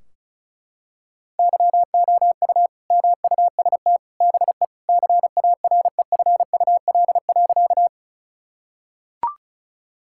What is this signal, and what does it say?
Morse code, heard as a keyed tone, about 34 wpm, 700 Hz. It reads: YOU MUST BE CAREFUL.
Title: Sentences from Top 1600 Words 35wpm